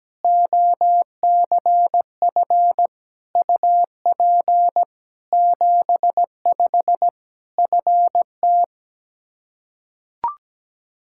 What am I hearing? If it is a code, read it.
Morse code, heard as a keyed tone, about 17 wpm, 700 Hz. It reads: OCF UP 75 FT
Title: Practice QSO - Exchange 2 - Info - 17wpm